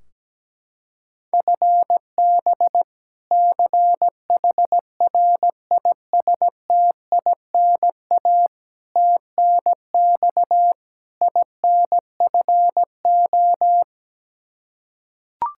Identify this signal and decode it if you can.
Morse code, heard as a keyed tone, about 17 wpm, 700 Hz. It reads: FB CHRISTINA TNX INFO
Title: Practice QSO - Exchange 3 - More Info - 17wpm